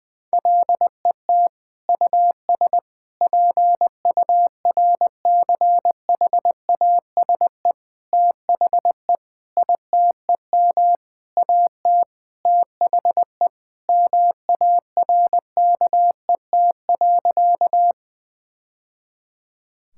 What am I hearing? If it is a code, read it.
Morse code, heard as a keyed tone, about 21 wpm, 700 Hz. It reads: LET US PURCHASE THE ITEM AT THE MARKET.
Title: Sentences from Top 1000 Words - Review Part 6 - 20wpm